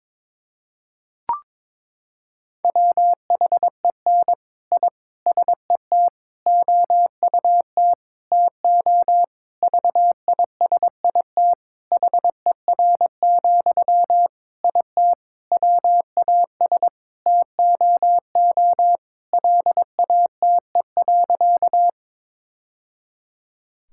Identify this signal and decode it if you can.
Morse code, heard as a keyed tone, about 22 wpm, 700 Hz. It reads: WHEN I SET OUT TO VISIT HER, IT WAS TOO LATE.
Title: Sentences from Top 1000 Words - Review Part 6 - 22wpm